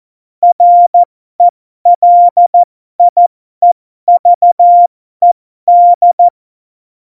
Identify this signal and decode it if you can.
Morse code, heard as a keyed tone, about 14 wpm, spaced as for 12 wpm, 700 Hz. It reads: RELIEVED
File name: Morsekurs / Witze